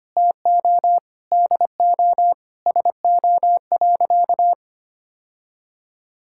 Morse code, heard as a keyed tone, about 25 wpm, 700 Hz. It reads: TO DO SO.
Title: Sentences from Top 1000 Words - Review Part 6 - 25wpm